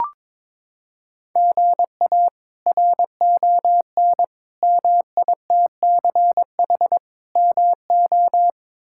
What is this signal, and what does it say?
Morse code, heard as a keyed tone, about 21 wpm, 700 Hz. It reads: GA RON MITCH MO
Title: K1USN SST QSO Practice - 22wpm